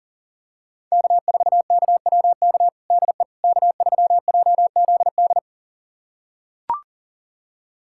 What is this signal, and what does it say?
Morse code, heard as a keyed tone, about 40 wpm, 700 Hz. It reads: K4KWK DE K3JZD
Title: USA Callsign DE Callsign - 5 Characters - 40wpm